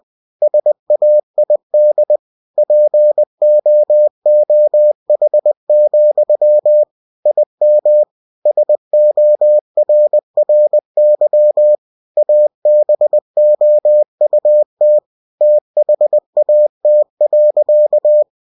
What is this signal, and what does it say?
Morse code, heard as a keyed tone, about 20 wpm, 600 Hz. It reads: SAID POOH, IM SORRY ABOUT THAT.
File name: cw-books / winnie-the-pooh / winnie-20wpm-page-0026